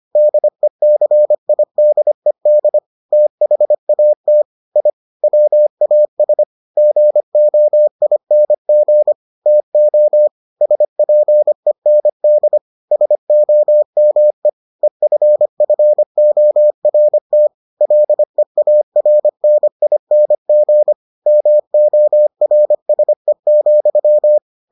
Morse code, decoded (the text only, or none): DECIDED THAT I WAS GOING TO SPEND SOME EFFORT LEARNING MORSE,